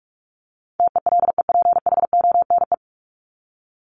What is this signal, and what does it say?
Morse code, heard as a keyed tone, about 44 wpm, 700 Hz. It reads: TELEPHONE
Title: Top 1501-1750 Words 45wpm